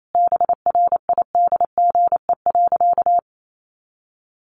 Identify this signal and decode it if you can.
Morse code, heard as a keyed tone, about 28 wpm, 700 Hz. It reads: BRIDGE.
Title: Sentences from Top 1200 Words 28wpm